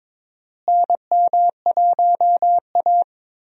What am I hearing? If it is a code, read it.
Morse code, heard as a keyed tone, about 22 wpm, 700 Hz. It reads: NM1A
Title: USA Call Signs - 4 Characters - Repeated 2x - 22wpm